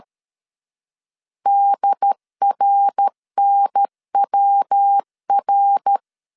Morse code, heard as a keyed tone, about 13 wpm, 800 Hz. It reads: DRNWR
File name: CW / DET / 1 3 1500C S C 12wpm 2750